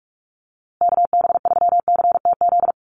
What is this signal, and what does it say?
Morse code, heard as a keyed tone, about 44 wpm, 690 Hz. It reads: KB3CTZ
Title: USA Call Signs Repeated 3x 45wpm